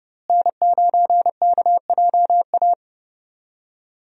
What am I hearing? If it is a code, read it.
Morse code, heard as a keyed tone, about 30 wpm, 700 Hz. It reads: N9KJA
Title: USA Call Signs - 5 Characters - Repeated 2x - 30wpm